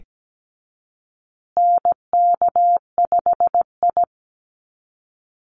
Morse code, heard as a keyed tone, about 17 wpm, 700 Hz. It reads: NK5I
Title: USA Call Signs Repeated 2x 17wpm